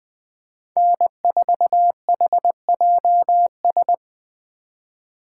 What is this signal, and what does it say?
Morse code, heard as a keyed tone, about 20 wpm, 700 Hz. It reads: N4HJS